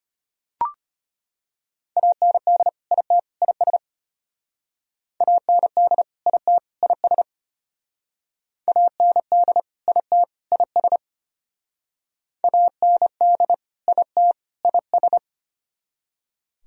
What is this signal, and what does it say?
Morse code, heard as a keyed tone, about 33 wpm, 700 Hz. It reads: AND IT IS AND IT IS AND IT IS AND IT IS